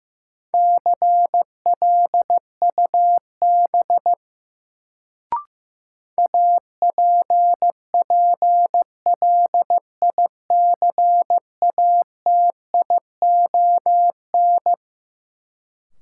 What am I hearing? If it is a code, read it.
Morse code, heard as a keyed tone, about 15 wpm, 700 Hz. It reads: CLUB APPLICATION
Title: Sentences from Top 1200 Words - New Word Drill Alt - 15wpm